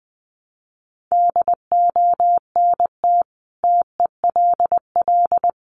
Morse code, heard as a keyed tone, about 20 wpm, 700 Hz. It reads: DONT TELL
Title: Sentences from Top 700 Words 20wpm